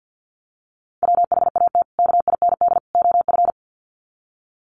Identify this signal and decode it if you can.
Morse code, heard as a keyed tone, about 48 wpm, 700 Hz. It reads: WHAT KIND OF
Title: Sets of 3 Words 50wpm